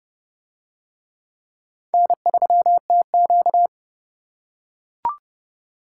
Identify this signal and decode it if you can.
Morse code, heard as a keyed tone, about 28 wpm, 700 Hz. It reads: N3TQ